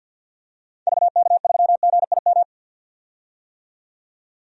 Morse code, heard as a keyed tone, about 51 wpm, 700 Hz. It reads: VK3GIM
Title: DXCC-100 Call Signs - 6 Characters - 50wpm